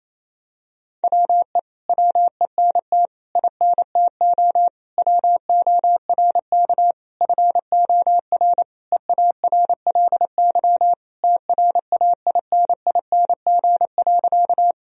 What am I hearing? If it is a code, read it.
Morse code, heard as a keyed tone, about 28 wpm, 700 Hz. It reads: WE WENT INTO WORK FOR EARLY TRAINING.
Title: Sentences from Top 1000 Words - Review Part 4 - 28wpm